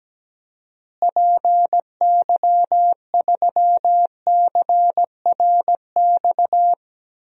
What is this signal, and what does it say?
Morse code, heard as a keyed tone, about 17 wpm, 700 Hz. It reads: PY3CRX
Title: DXCC-100 Call Signs - 6 Characters - Repeated 3x - 17wpm